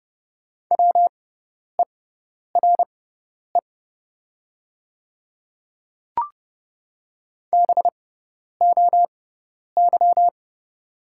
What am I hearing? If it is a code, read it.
Morse code, heard as a keyed tone, about 30 wpm, spaced as for 10 wpm, 690 Hz. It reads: WERE BOY
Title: 6x Character Spacing - Top 200 Words - 30wpm